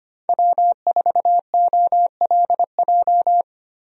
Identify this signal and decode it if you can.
Morse code, heard as a keyed tone, about 25 wpm, 700 Hz. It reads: W4OLJ
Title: USA Call Signs - 5 Characters - Repeated 2x - 25wpm